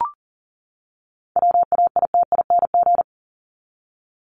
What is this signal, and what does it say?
Morse code, heard as a keyed tone, about 40 wpm, 700 Hz. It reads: WAITING
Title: ING Suffix - Top 100 Words with Base Words 40wpm